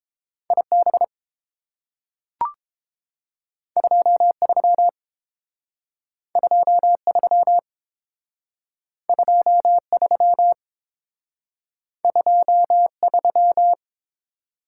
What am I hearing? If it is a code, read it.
Morse code, heard as a keyed tone, about 30 wpm, 700 Hz. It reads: IB 23 23 23 23